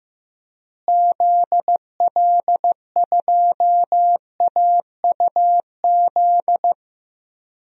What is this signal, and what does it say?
Morse code, heard as a keyed tone, about 16 wpm, 700 Hz. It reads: ZL2AUZ
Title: DXCC-100 Call Signs - 6 Characters - Repeated 3x - 15wpm